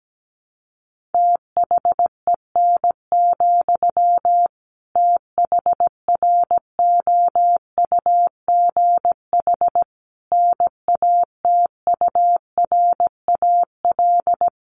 Morse code, heard as a keyed tone, about 17 wpm, 700 Hz. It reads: THEN, THROUGH NATURAL
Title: Sentences from Top 1400 Words 17wpm